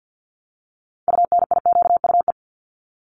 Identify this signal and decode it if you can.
Morse code, heard as a keyed tone, about 50 wpm, 700 Hz. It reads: UNIQUE
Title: Multi-letter Phonemes Repeated 3x 50wpm